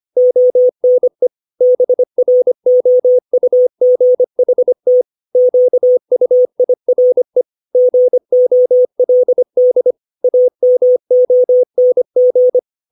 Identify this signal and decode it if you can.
Morse code, heard as a keyed tone, about 25 wpm, 500 Hz. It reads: ONE BROUGHT QUIRE GOLD AMONG